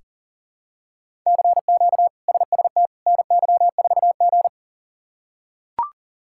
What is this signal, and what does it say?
Morse code, heard as a keyed tone, about 37 wpm, 700 Hz. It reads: CQ SST NY4G
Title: K1USN SST QSO Practice - 40wpm